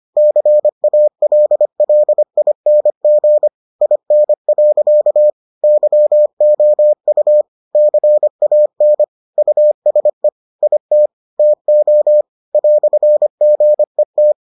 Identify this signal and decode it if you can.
Morse code, heard as a keyed tone, about 25 wpm, 600 Hz. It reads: CALLING IN. YOU CAN USE IT TO "GET